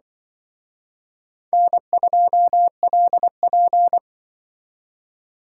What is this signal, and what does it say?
Morse code, heard as a keyed tone, about 25 wpm, 700 Hz. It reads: N2LP
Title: Speed Racing - USA Call Signs 20wpm